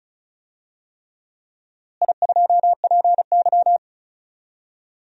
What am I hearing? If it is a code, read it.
Morse code, heard as a keyed tone, about 36 wpm, 700 Hz. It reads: I2PY
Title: DXCC-100 Call Signs - 4 Characters - Repeated 3x - 35wpm